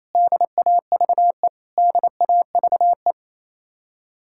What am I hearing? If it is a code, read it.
Morse code, heard as a keyed tone, about 35 wpm, 700 Hz. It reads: DAVE DAVE
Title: Speed-Racing Common QSO Phrases - 28wpm